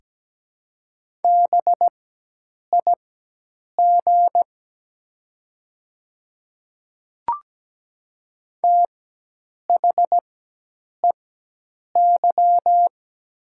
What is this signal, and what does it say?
Morse code, heard as a keyed tone, about 17 wpm, spaced as for 8 wpm, 700 Hz. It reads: BIG THEY
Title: 4x Character Spacing - Top 200 Words - 17wpm